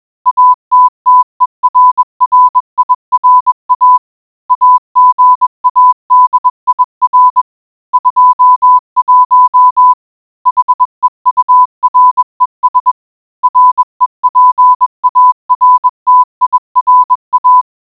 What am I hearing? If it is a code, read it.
Morse code, heard as a keyed tone, about 21 wpm, 1000 Hz. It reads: ATTERRIRA AGADIR 21 HEURES REPARTIRA